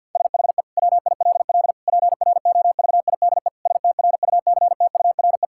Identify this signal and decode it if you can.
Morse code, heard as a keyed tone, about 52 wpm, 700 Hz. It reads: SHE WILL PROVIDE STRUCTURE
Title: Sentences from Top 1000 Words - Review Part 6 - 50wpm